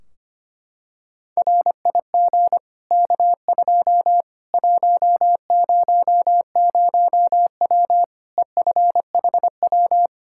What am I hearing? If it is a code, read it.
Morse code, heard as a keyed tone, about 25 wpm, 700 Hz. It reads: RIG K2 100W EFHW
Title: Practice QSO - Exchange 2 - Info - 25wpm